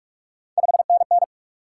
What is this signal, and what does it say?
Morse code, heard as a keyed tone, about 44 wpm, 700 Hz. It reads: 5NN
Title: Top 90 QSO Elements 45wpm